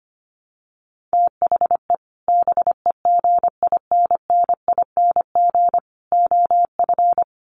Morse code, heard as a keyed tone, about 25 wpm, 700 Hz. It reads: THE BEGINNING OF
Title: Sets of 3 Words 25wpm